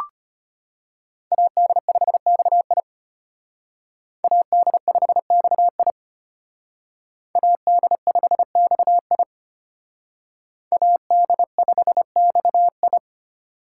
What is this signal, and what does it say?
Morse code, heard as a keyed tone, about 32 wpm, 700 Hz. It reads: AD5XI AD5XI AD5XI AD5XI